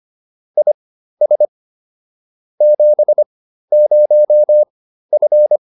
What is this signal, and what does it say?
Morse code, heard as a keyed tone, about 25 wpm, spaced as for 13 wpm, 600 Hz. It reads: IS 70F